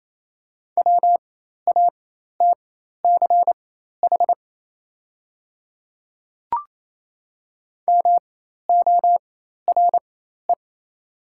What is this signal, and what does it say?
Morse code, heard as a keyed tone, about 28 wpm, spaced as for 13 wpm, 700 Hz. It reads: WATCH MORE
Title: 4x Character Spacing - Top 300 Words - 28wpm